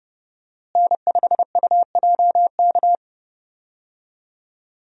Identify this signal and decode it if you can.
Morse code, heard as a keyed tone, about 30 wpm, 700 Hz. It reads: N5UJK
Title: USA Call Signs - 5 Characters - Repeated 3x - 30wpm